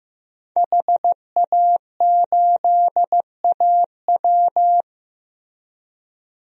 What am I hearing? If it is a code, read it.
Morse code, heard as a keyed tone, about 15 wpm, 700 Hz. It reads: HA8AW
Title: DXCC-100 Call Signs - 5 Characters - Repeated 3x - 15wpm